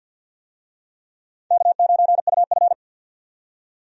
Morse code, heard as a keyed tone, about 53 wpm, 700 Hz. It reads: K9UF
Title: USA Call Signs - 4 Characters - 50wpm